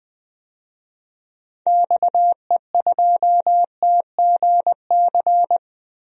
Morse code, heard as a keyed tone, about 20 wpm, 700 Hz. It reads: XE2TGC
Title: DXCC-100 Call Signs - 6 Characters - Repeated 3x - 20wpm